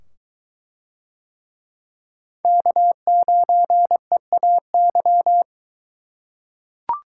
Morse code, heard as a keyed tone, about 21 wpm, 700 Hz. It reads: K9EAY